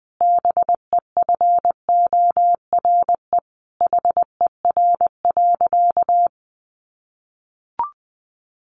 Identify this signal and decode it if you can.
Morse code, heard as a keyed tone, about 20 wpm, 700 Hz. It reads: BEFORE HER.
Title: Sentences from Top 1400 Words 20wpm